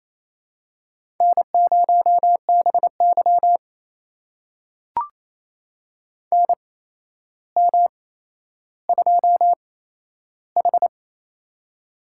Morse code, heard as a keyed tone, about 28 wpm, spaced as for 12 wpm, 700 Hz. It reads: N0BY NM2H
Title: USA Call Signs - 4 Characters - 8x Character Spacing - 28wpm